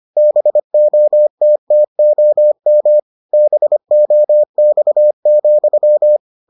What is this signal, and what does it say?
Morse code, heard as a keyed tone, about 25 wpm, 600 Hz. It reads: BOTTOM BOX,